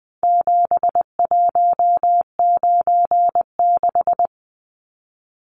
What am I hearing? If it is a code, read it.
Morse code, heard as a keyed tone, about 20 wpm, 700 Hz. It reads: 7196
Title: Numbers - 4-Digit 20wpm